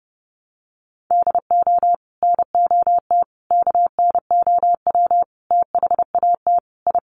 Morse code, heard as a keyed tone, about 30 wpm, 700 Hz. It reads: DO NOT KNOW THAT I
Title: Sets of 5 Words Encore2 30wpm